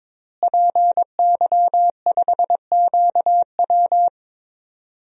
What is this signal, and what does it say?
Morse code, heard as a keyed tone, about 22 wpm, 700 Hz. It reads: PY5QW
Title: DXCC-100 Call Signs - 5 Characters - Repeated 3x - 22wpm